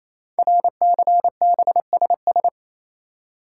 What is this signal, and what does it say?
Morse code, heard as a keyed tone, about 29 wpm, 700 Hz. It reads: RCBSS
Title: Code Groups - 5 Letters - 28wpm